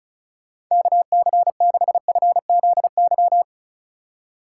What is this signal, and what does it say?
Morse code, heard as a keyed tone, about 46 wpm, 700 Hz. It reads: KC6FZY